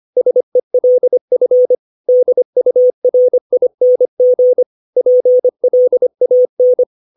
Morse code, heard as a keyed tone, about 25 wpm, 500 Hz. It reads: SELF DURING PLAN
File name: 632 words / 632 Words 25WPM